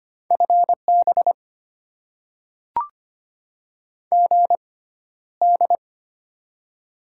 Morse code, heard as a keyed tone, about 25 wpm, spaced as for 13 wpm, 700 Hz. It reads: FB GD
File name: Top 90 QSO Elements - 6x Character Spacing - 25wpm